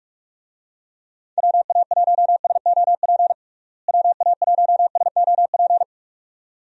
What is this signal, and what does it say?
Morse code, heard as a keyed tone, about 43 wpm, 700 Hz. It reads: WA1SOP WA1SOP
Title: USA Call Signs - 6 Characters - Repeated 2x - 45wpm